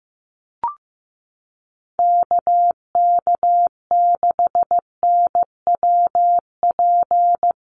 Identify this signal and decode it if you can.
Morse code, heard as a keyed tone, about 15 wpm, 700 Hz. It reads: KK6NWP
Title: USA Call Signs 15wpm